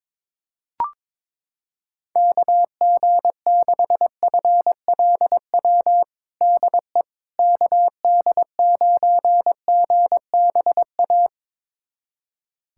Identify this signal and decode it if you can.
Morse code, heard as a keyed tone, about 22 wpm, 700 Hz. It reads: KG6FLW DE KD9GBA
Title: USA Callsign DE Callsign 22wpm